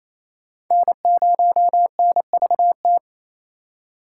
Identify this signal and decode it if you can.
Morse code, heard as a keyed tone, about 28 wpm, 700 Hz. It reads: N0NVT
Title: USA Call Signs - 5 Characters - 4x Character Spacing - 28wpm